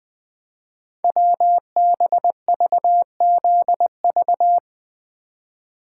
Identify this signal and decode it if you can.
Morse code, heard as a keyed tone, about 20 wpm, 700 Hz. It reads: WBVZV
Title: Code Groups - 5 Letters - 20wpm